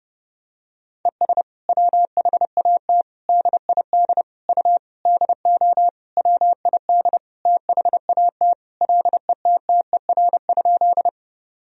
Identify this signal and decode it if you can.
Morse code, heard as a keyed tone, about 32 wpm, 700 Hz. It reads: ES WHAT DID U DO WID THAT LETTER?
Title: Sentences from Top 300 Words and Abbreviations 30wpm